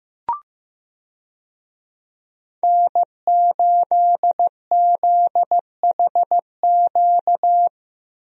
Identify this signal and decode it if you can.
Morse code, heard as a keyed tone, about 15 wpm, 700 Hz. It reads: N8ZHQ
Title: USA Call Signs - 5 Characters - 15wpm